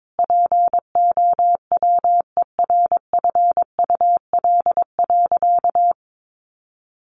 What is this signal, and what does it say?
Morse code, heard as a keyed tone, about 22 wpm, 700 Hz. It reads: POWERFUL.
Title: Sentences from Top 1300 Words 22wpm